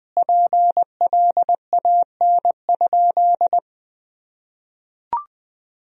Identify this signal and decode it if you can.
Morse code, heard as a keyed tone, about 21 wpm, 700 Hz. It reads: PLAN?
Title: Sentences from Top 500 Words and Abbreviations 20wpm